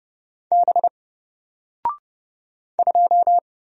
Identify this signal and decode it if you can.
Morse code, heard as a keyed tone, about 31 wpm, 700 Hz. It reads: B 2
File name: Single-Letter-Number-30wpm